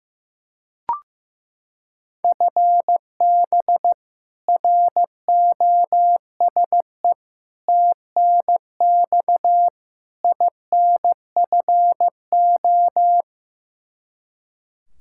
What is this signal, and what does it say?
Morse code, heard as a keyed tone, about 15 wpm, 700 Hz. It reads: FB ROSE TNX INFO
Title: Practice QSO - Exchange 3 - More Info - 15wpm